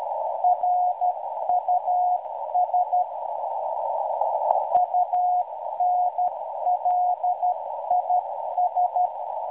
Morse code, heard as a keyed tone, about 12 wpm, 720 Hz, with noise. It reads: RUS UNLIS